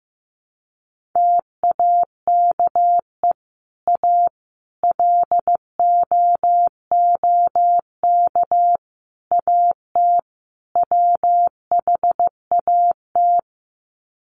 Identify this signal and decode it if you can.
Morse code, heard as a keyed tone, about 14 wpm, 700 Hz. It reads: TAKE A LOOK AT WHAT
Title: Sets of 5 Words Encore2 15wpm